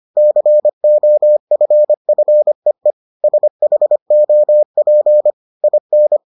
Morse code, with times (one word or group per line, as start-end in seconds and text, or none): 0.2-2.9 COFFEE
3.2-5.3 SHOP
5.6-6.2 IN